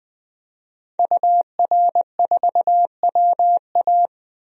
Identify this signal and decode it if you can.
Morse code, heard as a keyed tone, about 20 wpm, 700 Hz. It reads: UR4WA
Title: DXCC-100 Call Signs - 5 Characters - 20wpm